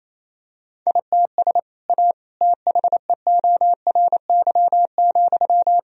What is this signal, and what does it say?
Morse code, heard as a keyed tone, about 28 wpm, 700 Hz. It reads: ITS A THEORY,
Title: Sentences from Top 1000 Words - Review Part 3 - 28wpm